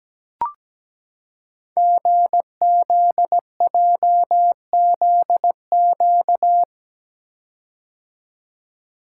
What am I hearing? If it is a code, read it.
Morse code, heard as a keyed tone, about 17 wpm, 700 Hz. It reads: GZJZQ
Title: Code Groups - 5 Letter:Numbers - 17wpm